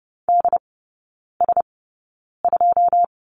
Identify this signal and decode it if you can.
Morse code, heard as a keyed tone, about 30 wpm, spaced as for 9 wpm, 700 Hz. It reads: DS2
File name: 081 Lesson 14e - Number 2 - Three Characters 30wpm